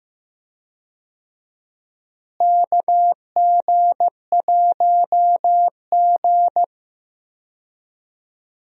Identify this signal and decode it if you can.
Morse code, heard as a keyed tone, about 15 wpm, 700 Hz. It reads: KG1G
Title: USA Call Signs - 4 Characters - Repeated 2x - 15wpm